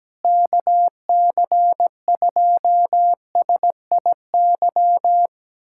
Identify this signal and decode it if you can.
Morse code, heard as a keyed tone, about 17 wpm, 700 Hz. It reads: KC2SIY
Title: USA Callsign DE Callsign 17wpm